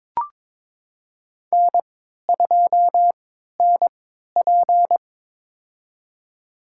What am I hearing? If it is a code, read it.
Morse code, heard as a keyed tone, about 22 wpm, spaced as for 13 wpm, 700 Hz. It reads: N2NP
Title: USA Call Signs - 4 Characters - 3x Character Spacing - 22wpm